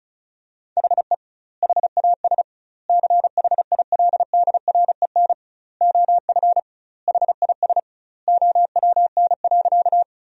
Morse code, heard as a keyed tone, about 36 wpm, 700 Hz. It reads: HE HAS CHILDREN OF HIS OWN.
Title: 2x Word-Spacing - Sentences from Top 300 Words - 35wpm